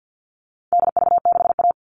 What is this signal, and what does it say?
Morse code, heard as a keyed tone, about 50 wpm, 700 Hz. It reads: DV6A
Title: All Cut Numbers - 4-Digit 50wpm